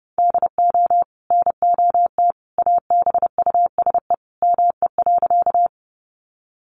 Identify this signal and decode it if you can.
Morse code, heard as a keyed tone, about 30 wpm, 700 Hz. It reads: DO NOT ABUSE ME.
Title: Sentences from Top 1300 Words 30wpm